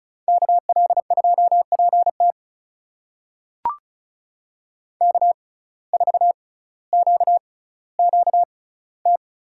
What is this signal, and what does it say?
Morse code, heard as a keyed tone, about 34 wpm, spaced as for 18 wpm, 700 Hz. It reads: KL2PT K4QQT
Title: USA Call Signs - 5 Characters - 6x Character Spacing - 35wpm